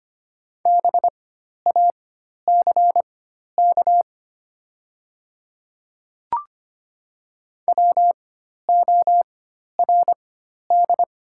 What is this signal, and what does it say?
Morse code, heard as a keyed tone, about 25 wpm, spaced as for 12 wpm, 700 Hz. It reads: BACK WORD
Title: 4x Character Spacing - Top 200 Words - 25wpm